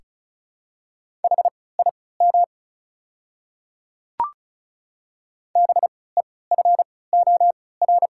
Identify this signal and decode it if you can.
Morse code, heard as a keyed tone, about 36 wpm, spaced as for 20 wpm, 700 Hz. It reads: HIM BEFOR
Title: 3x Character Spacing - Top 200 Words - 35wpm